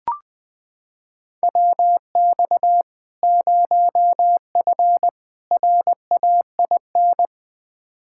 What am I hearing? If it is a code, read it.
Morse code, heard as a keyed tone, about 20 wpm, 700 Hz. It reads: WX 0F RAIN
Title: Practice QSO - Exchange 2 - Info - 20wpm